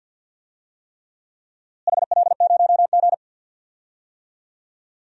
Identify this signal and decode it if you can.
Morse code, heard as a keyed tone, about 50 wpm, 700 Hz. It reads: HL0G
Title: DXCC-100 Call Signs - 4 Characters - Repeated 3x - 50wpm